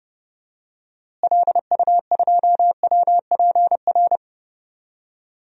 Morse code, heard as a keyed tone, about 30 wpm, 700 Hz. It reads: LU2WPR
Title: DXCC-100 Call Signs - 6 Characters - Repeated 2x - 30wpm